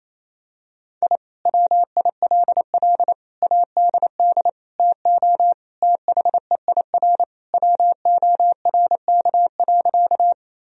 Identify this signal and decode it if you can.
Morse code, heard as a keyed tone, about 28 wpm, 700 Hz. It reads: I WILL ADD TO THEIR WORK.
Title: Sentences from Top 1000 Words - Review Part 1 - 28wpm